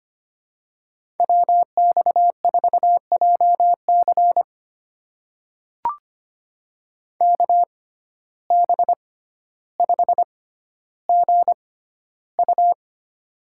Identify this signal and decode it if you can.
Morse code, heard as a keyed tone, about 25 wpm, spaced as for 13 wpm, 700 Hz. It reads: WX4JC KB5GU
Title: USA Call Signs - 6x Character Spacing - 25wpm